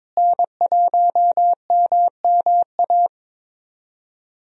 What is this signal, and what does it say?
Morse code, heard as a keyed tone, about 22 wpm, 700 Hz. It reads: N1MMA
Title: USA Call Signs - 5 Characters - Repeated 2x - 22wpm